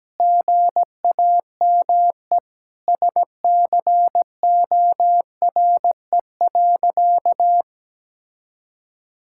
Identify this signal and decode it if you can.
Morse code, heard as a keyed tone, about 17 wpm, 700 Hz. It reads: GAME SCORE.
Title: Sentences from Top 1600 Words 17wpm